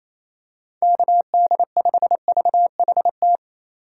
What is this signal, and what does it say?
Morse code, heard as a keyed tone, about 28 wpm, 700 Hz. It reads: KD5VHT
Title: USA Call Signs - 6 Characters - 4x Character Spacing - 28wpm